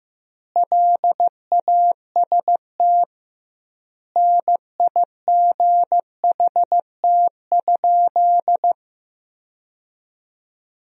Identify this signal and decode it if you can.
Morse code, heard as a keyed tone, about 15 wpm, 700 Hz. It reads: LAST NIGHT?
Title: 2x Word-Spacing - Sentences from Top 300 Words - 15wpm